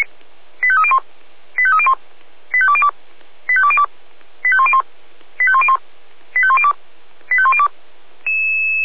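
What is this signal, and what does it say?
Five-tone selective call (ZVEI), heard as quick runs of five tones, tones of about 70 ms. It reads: 85301 85301 85202 85202 85101 85101 85102 85102
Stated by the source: ZVEI five-tone call received on an SDR, fire brigade Orlamünde (feuerwehr-orlamuende) recording archive.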